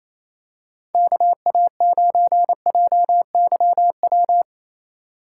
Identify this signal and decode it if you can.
Morse code, heard as a keyed tone, about 28 wpm, 700 Hz. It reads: KA9JYW
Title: USA Call Signs - 6 Characters - 3x Character Spacing - 28wpm